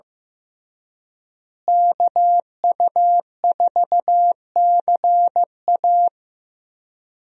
Morse code, heard as a keyed tone, about 15 wpm, 700 Hz. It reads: KU4CA
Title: USA Call Signs - 5 Characters - Repeated 3x - 15wpm